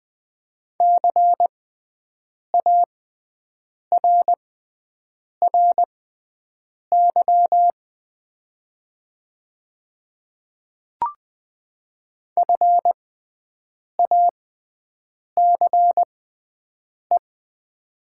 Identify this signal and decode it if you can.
Morse code, heard as a keyed tone, about 20 wpm, spaced as for 7 wpm, 700 Hz. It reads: CARRY FACE
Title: 6x Character Spacing - Top 300 Words - 20wpm